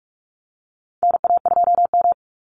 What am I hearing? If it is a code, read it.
Morse code, heard as a keyed tone, about 44 wpm, 700 Hz. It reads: NA2M
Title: Call Signs - 45wpm